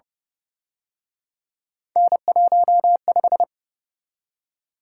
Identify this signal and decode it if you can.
Morse code, heard as a keyed tone, about 30 wpm, 700 Hz. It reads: N15
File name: Speed-Racing - Three Letter-Number Combinations - 25wpm